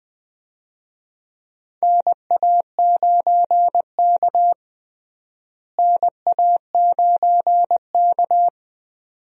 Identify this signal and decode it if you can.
Morse code, heard as a keyed tone, about 21 wpm, 700 Hz. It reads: NA9K NA9K
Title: USA Call Signs - 4 Characters - Repeated 2x - 20wpm